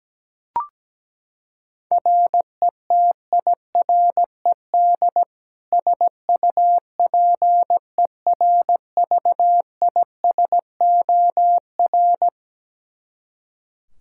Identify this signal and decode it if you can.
Morse code, heard as a keyed tone, about 17 wpm, 700 Hz. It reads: RETIRED SUPERVISOR
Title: K1USN SST QSO Practice 17wpm